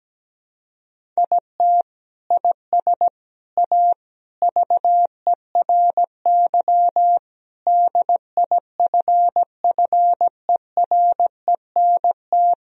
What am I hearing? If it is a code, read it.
Morse code, heard as a keyed tone, about 17 wpm, 700 Hz. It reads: IT IS A VERY DIFFERENT
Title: Sentences from Top 1000 Words - Review Part 4 - 17wpm